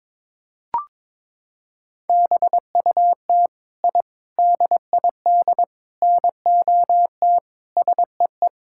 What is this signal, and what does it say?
Morse code, heard as a keyed tone, about 22 wpm, 700 Hz. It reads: BUT I DID NOT SEE
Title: Sentences from Top 1000 Words - Review Part 5 - 22wpm